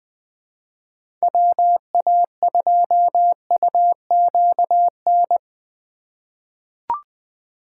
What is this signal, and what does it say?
Morse code, heard as a keyed tone, about 20 wpm, 700 Hz. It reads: WA2UQN
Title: USA Call Signs - 6 Characters - 3x Character Spacing - 20wpm